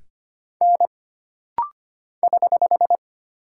Morse code, heard as a keyed tone, about 25 wpm, 700 Hz. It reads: N <HH>
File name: Single Character (Letter:Number:Punctuation:Prosign) - Rapid-Fire - 25wpm